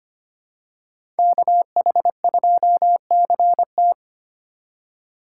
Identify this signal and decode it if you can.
Morse code, heard as a keyed tone, about 25 wpm, 700 Hz. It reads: KH2CT
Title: USA Call Signs - 5 Characters - 4x Character Spacing - 25wpm